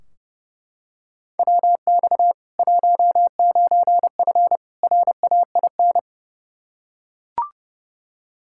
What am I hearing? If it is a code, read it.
Morse code, heard as a keyed tone, about 30 wpm, 700 Hz. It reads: WX 19F RAIN
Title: K1USN SST QSO Practice 30wpm